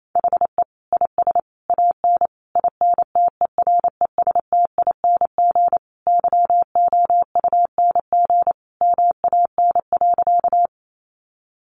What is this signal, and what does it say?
Morse code, heard as a keyed tone, about 28 wpm, 700 Hz. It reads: HE IS AN INTERESTING YOUNG MAN.
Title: Sentences from Top 1000 Words 28wpm